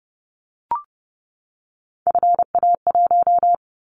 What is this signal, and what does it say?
Morse code, heard as a keyed tone, about 30 wpm, 700 Hz. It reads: FA1
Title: Callsign Prefixes for Top 25 DXCC - 30wpm